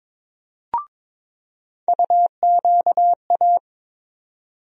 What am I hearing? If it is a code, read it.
Morse code, heard as a keyed tone, about 22 wpm, 700 Hz. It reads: UQA
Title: USA Call Sign Postfixes - 3 Letters 22wpm